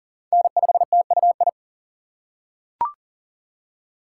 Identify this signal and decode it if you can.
Morse code, heard as a keyed tone, about 40 wpm, 700 Hz. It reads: N5TUI
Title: USA Call Signs - 5 Characters - 3x Character Spacing - 40wpm